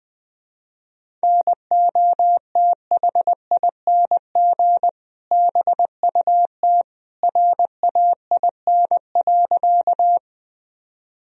Morse code, heard as a keyed tone, about 21 wpm, 700 Hz. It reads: NOTHING BUT RAIN.
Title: Sentences from Top 1000 Words - Review Part 1 - 20wpm